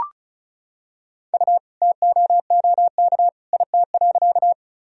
Morse code, heard as a keyed tone, about 36 wpm, 700 Hz. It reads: U TOOK IT.
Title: Sentences from Top 300 Words and Abbreviations 35wpm